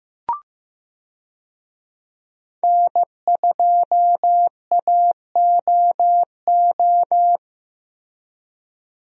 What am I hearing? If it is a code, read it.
Morse code, heard as a keyed tone, about 15 wpm, 700 Hz. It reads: N2AOO
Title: USA Call Signs - 5 Characters - Repeated 2x - 15wpm